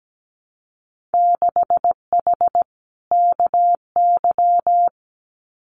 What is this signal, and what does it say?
Morse code, heard as a keyed tone, about 17 wpm, 700 Hz. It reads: 6H KY
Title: Winter Field Day Exchanges Encore 17wpm